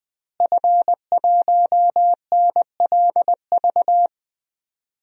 Morse code, heard as a keyed tone, about 20 wpm, 700 Hz. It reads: F1NLV
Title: DXCC-100 Call Signs - 5 Characters - Repeated 3x - 20wpm